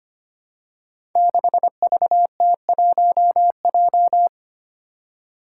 Morse code, heard as a keyed tone, about 25 wpm, 700 Hz. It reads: 6VT1J
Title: Five Challenging Letter-Number Combinations 25wpm